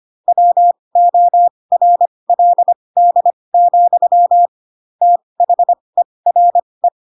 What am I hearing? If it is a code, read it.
Morse code, heard as a keyed tone, about 25 wpm, spaced as for 20 wpm, 700 Hz. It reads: WORLD, THERE
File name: Pinocchio / Pinocchio ch4 15 25 ik1hge